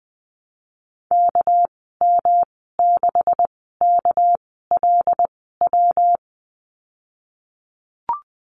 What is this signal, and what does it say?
Morse code, heard as a keyed tone, about 20 wpm, spaced as for 14 wpm, 700 Hz. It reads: KM6KLW
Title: 219 Lesson 35g - Number 0 - Call Signs 20wpm